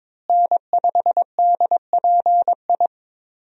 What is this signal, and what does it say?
Morse code, heard as a keyed tone, about 22 wpm, 700 Hz. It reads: N5DPI
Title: USA Call Signs - 5 Characters - Repeated 3x - 22wpm